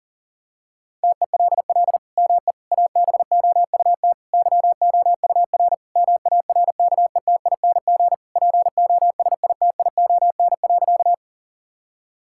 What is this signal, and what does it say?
Morse code, heard as a keyed tone, about 37 wpm, 700 Hz. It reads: TELL ME ABOUT YOUR MARKETING POSITION.
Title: Sentences from Top 1000 Words - Review Part 5 - 40wpm